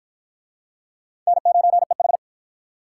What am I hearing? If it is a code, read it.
Morse code, heard as a keyed tone, about 48 wpm, 700 Hz. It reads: N9EH